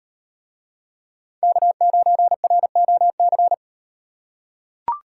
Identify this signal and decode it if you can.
Morse code, heard as a keyed tone, about 38 wpm, 700 Hz. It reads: K9ROC